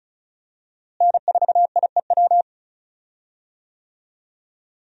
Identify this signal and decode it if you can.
Morse code, heard as a keyed tone, about 36 wpm, 700 Hz. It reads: N4IEW
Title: USA Call Signs - 5 Characters - Repeated 2x - 35wpm